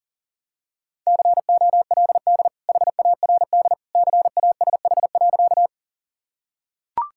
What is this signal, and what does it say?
Morse code, heard as a keyed tone, about 40 wpm, 700 Hz. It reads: COLD HARD CASH.
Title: Sentences from Top 1000 Words - Review Part 5 - 40wpm